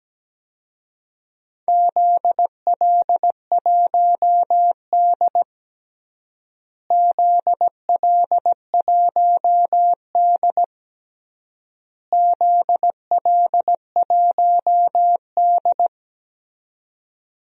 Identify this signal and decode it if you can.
Morse code, heard as a keyed tone, about 17 wpm, 700 Hz. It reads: ZL1D ZL1D ZL1D
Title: DXCC-100 Call Signs - 4 Characters - Repeated 3x - 17wpm